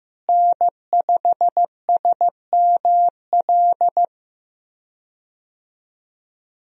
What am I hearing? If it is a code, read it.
Morse code, heard as a keyed tone, about 15 wpm, 700 Hz. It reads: N5SML
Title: USA Call Signs - 5 Characters - Repeated 2x - 15wpm